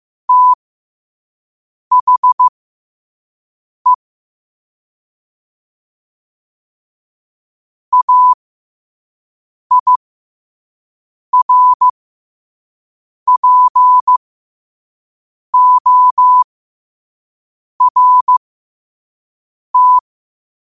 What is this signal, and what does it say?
Morse code, heard as a keyed tone, about 15 wpm, spaced as for 5 wpm, 1000 Hz. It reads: THE AIRPORT
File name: qotdfiles / 5wpm